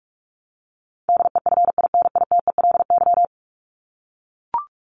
Morse code, heard as a keyed tone, about 44 wpm, 700 Hz. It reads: DEFINITELY
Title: Top 1751-2000 Words 45wpm